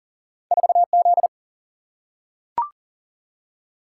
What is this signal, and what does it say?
Morse code, heard as a keyed tone, about 38 wpm, 700 Hz. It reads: 4Z
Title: Single-Letter Single-Number 40wpm